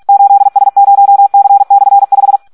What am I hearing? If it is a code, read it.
Morse code, heard as a keyed tone, about 46 wpm, 800 Hz. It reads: 8S0C/5
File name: Audio practica CW / SD 1 46